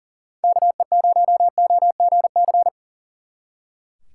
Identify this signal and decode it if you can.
Morse code, heard as a keyed tone, about 37 wpm, 700 Hz. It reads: KE0OGC
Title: Speed Racing - USA 2x3 Call Signs 40wpm